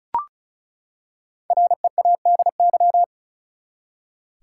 Morse code, heard as a keyed tone, about 34 wpm, 700 Hz. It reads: READY
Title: Five Letter Words - 35wpm